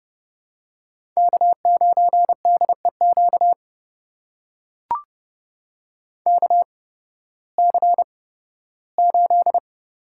Morse code, heard as a keyed tone, about 30 wpm, spaced as for 16 wpm, 700 Hz. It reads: K9DEQ KC8
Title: USA Call Signs - 8x Character Spacing - 30wpm